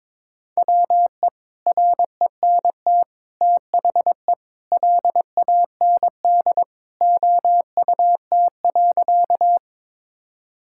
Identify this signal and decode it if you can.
Morse code, heard as a keyed tone, about 22 wpm, 700 Hz. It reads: WE RENT THE LAND OUT.
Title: Sentences from Top 1000 Words - Review Part 5 - 22wpm